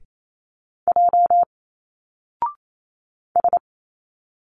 Morse code, heard as a keyed tone, about 28 wpm, 700 Hz. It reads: J S
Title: Single Letters 28wpm